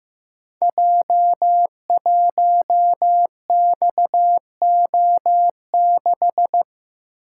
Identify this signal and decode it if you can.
Morse code, heard as a keyed tone, about 15 wpm, 700 Hz. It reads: J1XO6
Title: Code Groups - 5 Letter:Numbers - 15wpm